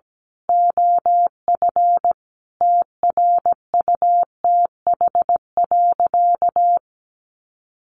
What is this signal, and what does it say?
Morse code, heard as a keyed tone, about 17 wpm, 700 Hz. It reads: OF TRUTH.
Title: Sentences from Top 1500 Words 17wpm